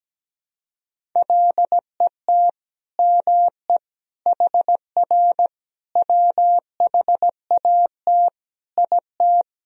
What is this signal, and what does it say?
Morse code, heard as a keyed tone, about 17 wpm, 700 Hz. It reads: LET ME HR WHAT IT
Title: Sentences from Top 300 Words and Abbreviations 17wpm